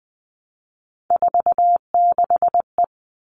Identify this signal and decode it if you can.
Morse code, heard as a keyed tone, about 20 wpm, 700 Hz. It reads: 46E
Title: All Cut Numbers - 3-Digit 20wpm